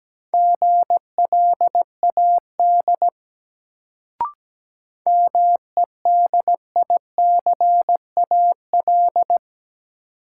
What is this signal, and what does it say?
Morse code, heard as a keyed tone, about 17 wpm, 700 Hz. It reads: GLAD MEDICAL
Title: Sentences from Top 800 Words - New Word Drill - 17wpm